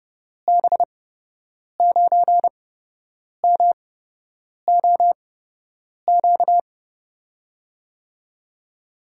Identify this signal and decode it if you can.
Morse code, heard as a keyed tone, about 31 wpm, spaced as for 8 wpm, 700 Hz. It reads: B9MOQ
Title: USA Call Signs - 8x Character Spacing - 30wpm